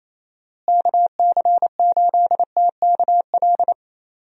Morse code, heard as a keyed tone, about 27 wpm, 700 Hz. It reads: KC8TKL
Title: USA Call Signs - 6 Characters - Repeated 3x - 28wpm